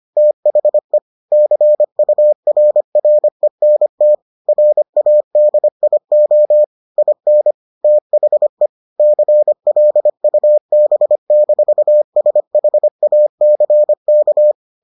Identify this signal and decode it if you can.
Morse code, heard as a keyed tone, about 25 wpm, 600 Hz. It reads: THE CURRENT RADIO IN THE CLUB-SHACK